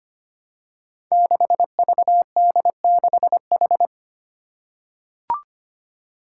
Morse code, heard as a keyed tone, about 25 wpm, 700 Hz. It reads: 6VD6H
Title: Five Challenging Letter-Number Combinations 25wpm